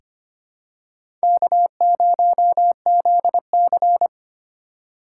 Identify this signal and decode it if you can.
Morse code, heard as a keyed tone, about 25 wpm, 700 Hz. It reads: K0ZC
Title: USA Call Signs - 4 Characters - 4x Character Spacing - 25wpm